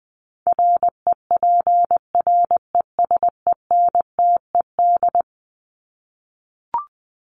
Morse code, heard as a keyed tone, about 20 wpm, 700 Hz. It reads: REPRESENTED
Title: ED Suffix - Top 100 Words with Base Words 20wpm